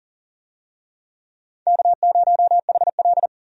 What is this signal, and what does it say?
Morse code, heard as a keyed tone, about 40 wpm, 690 Hz. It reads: K0HL
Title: USA Call Signs - 4 Characters - 40wpm